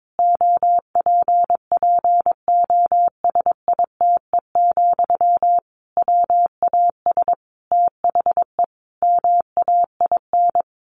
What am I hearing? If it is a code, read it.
Morse code, heard as a keyed tone, about 22 wpm, 700 Hz. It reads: OPPOSITE, WAS THE MAIN
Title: Sentences from Top 1300 Words 22wpm